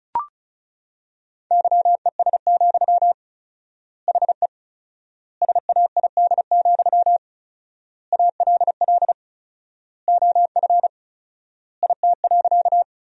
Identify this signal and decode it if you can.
Morse code, heard as a keyed tone, about 34 wpm, 700 Hz. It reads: YES, HE SAID, ALL OF IT.
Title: 4x Word-Spacing - Sentences from Top 500 Words - 35wpm